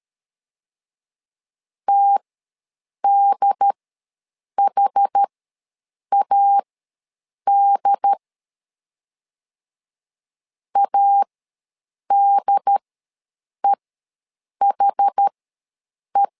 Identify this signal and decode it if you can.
Morse code, heard as a keyed tone, about 13 wpm, spaced as for 7 wpm, 800 Hz. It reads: TDHAD ADEHE
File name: LEZIONI / 2 UHDN / 2 UHD ws 12wpm 1722